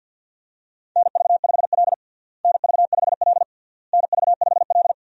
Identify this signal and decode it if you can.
Morse code, heard as a keyed tone, about 52 wpm, 700 Hz. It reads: NV5L NV5L NV5L
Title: USA Call Signs - 4 Characters - Repeated 3x - 50wpm